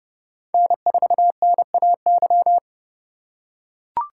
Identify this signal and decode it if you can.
Morse code, heard as a keyed tone, about 30 wpm, 700 Hz. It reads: N4NAY